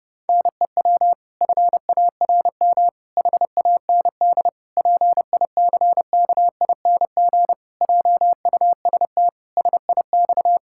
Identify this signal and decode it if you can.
Morse code, heard as a keyed tone, about 32 wpm, 700 Hz. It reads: NEW FARM HAND PICKING JUST SIX
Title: Pangram Challenge v2 - 30wpm